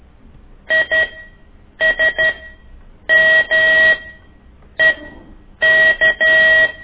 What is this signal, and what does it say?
Morse code, heard as a keyed tone, about 12 wpm, spaced as for 7 wpm, 1900 Hz. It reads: ISMEK
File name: ik1weg2 / CORSI CW